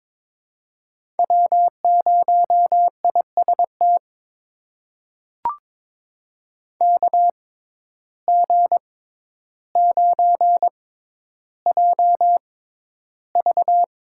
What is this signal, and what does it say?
Morse code, heard as a keyed tone, about 23 wpm, spaced as for 11 wpm, 700 Hz. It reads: W0IST KG9JV
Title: USA Call Signs - 5 Characters - 6x Character Spacing - 22wpm